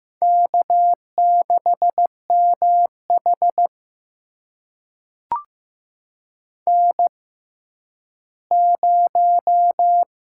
Morse code, heard as a keyed tone, about 15 wpm, spaced as for 10 wpm, 700 Hz. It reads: K6MH N0
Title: USA Call Signs - 4 Characters - 6x Character Spacing - 15wpm